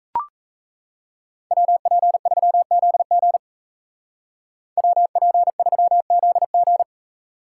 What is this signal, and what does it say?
Morse code, heard as a keyed tone, about 33 wpm, 700 Hz. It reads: WP3ZG WP3ZG
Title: Speed Racing - USA Call Signs 28wpm